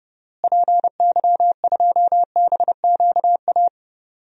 Morse code, heard as a keyed tone, about 30 wpm, 700 Hz. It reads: PY2BQA
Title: DXCC-100 Call Signs - 6 Characters - Repeated 3x - 30wpm